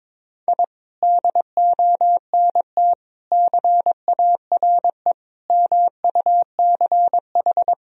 Morse code, heard as a keyed tone, about 22 wpm, 700 Hz. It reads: I DONT CARE MUCH
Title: Sentences from Top 1000 Words - Review Part 4 - 22wpm